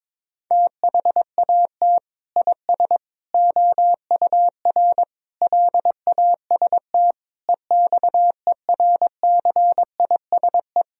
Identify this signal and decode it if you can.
Morse code, heard as a keyed tone, about 22 wpm, 700 Hz. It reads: THAT IS OUR LAST EXERCISE
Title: Sentences from Top 1000 Words - Review Part 4 - 22wpm